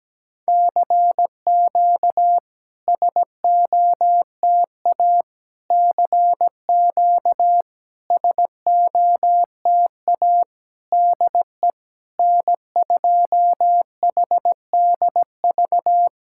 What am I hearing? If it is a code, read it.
Morse code, heard as a keyed tone, about 17 wpm, 700 Hz. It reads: CQ SOTA CQ SOTA DE N2HDV
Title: SOTA QSO Practice 17wpm